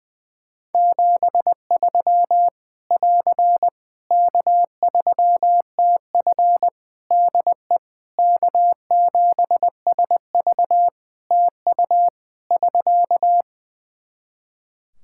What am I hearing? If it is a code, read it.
Morse code, heard as a keyed tone, about 20 wpm, 700 Hz. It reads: 73 <AR> K3TF DE K7SV TU <SK>
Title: Practice QSO - Exchange 4 - Ending - 20wpm